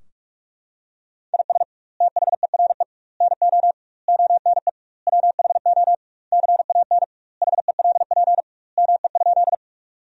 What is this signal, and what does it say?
Morse code, heard as a keyed tone, about 41 wpm, 700 Hz. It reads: IS THERE NO ONE WHO CAN HELP ME?
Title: 2x Word-Spacing - Sentences from Top 200 Words - 45wpm